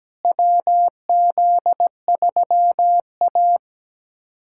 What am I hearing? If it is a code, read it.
Morse code, heard as a keyed tone, about 17 wpm, 700 Hz. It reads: WZ3A
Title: USA Callsign DE Callsign - 4 Characters - 17wpm